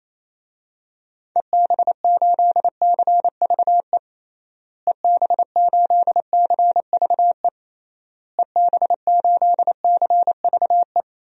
Morse code, heard as a keyed tone, about 28 wpm, 700 Hz. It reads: EB8CVE EB8CVE EB8CVE
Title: DXCC-100 Call Signs - 6 Characters - Repeated 3x - 28wpm